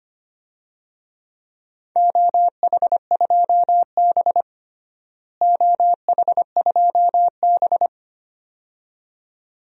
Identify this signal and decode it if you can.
Morse code, heard as a keyed tone, about 25 wpm, 700 Hz. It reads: OH2B OH2B
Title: DXCC-100 Call Signs - 4 Characters - Repeated 2x - 25wpm